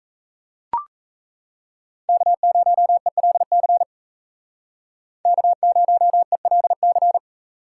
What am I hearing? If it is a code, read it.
Morse code, heard as a keyed tone, about 37 wpm, 700 Hz. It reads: K0ELC K0ELC